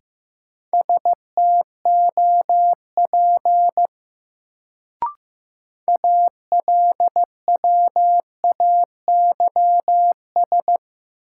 Morse code, heard as a keyed tone, about 15 wpm, 700 Hz. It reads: STOP ALWAYS
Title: Sentences from Top 300 Words - New Word Drill - 15wpm